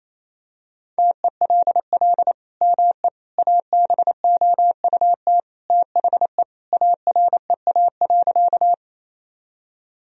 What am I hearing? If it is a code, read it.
Morse code, heard as a keyed tone, about 28 wpm, 700 Hz. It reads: TELL ME ABOUT THE AREA.
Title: Sentences from Top 1000 Words - Review Part 2 - 28wpm